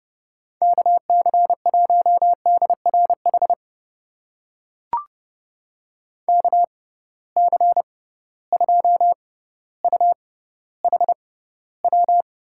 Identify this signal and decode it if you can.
Morse code, heard as a keyed tone, about 31 wpm, spaced as for 15 wpm, 700 Hz. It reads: KC1DRH KC2UHW
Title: USA Call Signs - 6 Characters - 6x Character Spacing - 30wpm